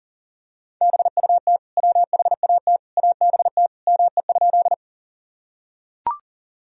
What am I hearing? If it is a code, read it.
Morse code, heard as a keyed tone, about 40 wpm, 700 Hz. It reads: BUT WHAT ABT ME?
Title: Sentences from Top 200 Words and Abbreviations 40wpm